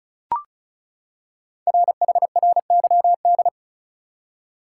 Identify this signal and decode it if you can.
Morse code, heard as a keyed tone, about 36 wpm, 690 Hz. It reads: RHRYD
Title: Code Groups - 5 Letters - 35wpm